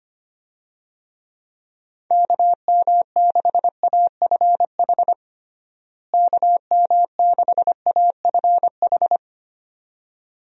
Morse code, heard as a keyed tone, about 25 wpm, 700 Hz. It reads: KM6AFH KM6AFH
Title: USA Call Signs - 6 Characters - Repeated 2x - 25wpm